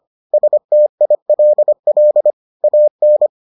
Morse code, heard as a keyed tone, about 25 wpm, 600 Hz. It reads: STILL AN